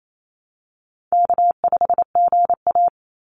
Morse code, heard as a keyed tone, about 28 wpm, 700 Hz. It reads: K5GA
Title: Easy Call Signs Repeated 3x - 28wpm